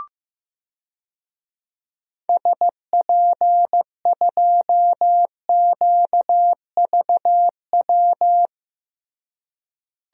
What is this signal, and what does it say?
Morse code, heard as a keyed tone, about 15 wpm, 700 Hz. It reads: SP2QVW
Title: DXCC-100 Call Signs - 6 Characters - Repeated 3x - 15wpm